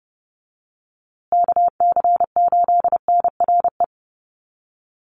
Morse code, heard as a keyed tone, about 30 wpm, 700 Hz. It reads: KC8NRE
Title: USA Call Signs Repeated 3x 30wpm